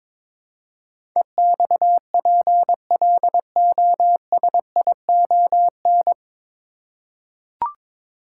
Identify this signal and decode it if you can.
Morse code, heard as a keyed tone, about 22 wpm, 700 Hz. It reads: EXPLOSION
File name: 9 Letter Words - First 3 Letters Given - 22wpm